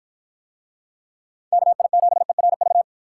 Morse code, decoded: KI7ERV

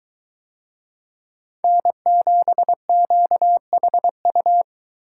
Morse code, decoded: N7QHU

